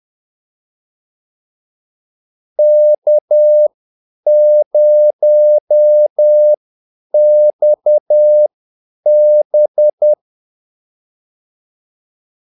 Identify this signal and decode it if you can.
Morse code, heard as a keyed tone, about 10 wpm, spaced as for 8 wpm, 600 Hz. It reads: K0XB
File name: Callsigns-10-8wpm